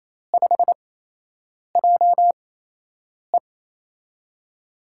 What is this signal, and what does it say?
Morse code, heard as a keyed tone, about 28 wpm, spaced as for 8 wpm, 700 Hz. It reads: 5JE
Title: USA Call Signs - 5 Characters - 8x Character Spacing - 28wpm